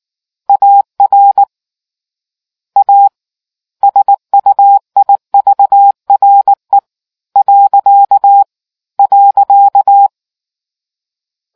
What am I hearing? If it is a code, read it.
Morse code, heard as a keyed tone, about 19 wpm, 800 Hz. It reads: AR A SUIVRE .